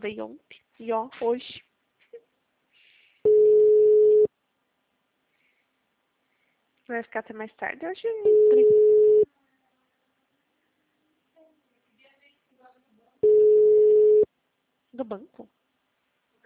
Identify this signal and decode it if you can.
Ringback tone (Europe, 425 Hz), with speech.
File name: out- -2024- - - .93345